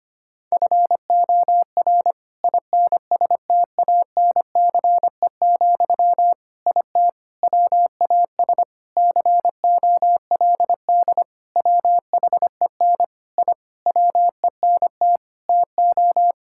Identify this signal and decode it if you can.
Morse code, heard as a keyed tone, about 25 wpm, 700 Hz. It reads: FOR INSTANCE, IT WAS COLD WHEN I WENT TO
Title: Sentences from Top 1000 Words - Review Part 5 - 25wpm